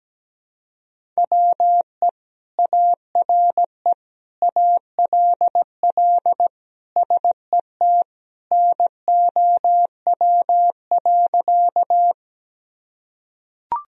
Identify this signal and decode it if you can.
Morse code, heard as a keyed tone, about 17 wpm, 700 Hz. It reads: WE ARE ALL SET NOW.
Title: Sentences from Top 1000 Words - Review Part 1 - 17wpm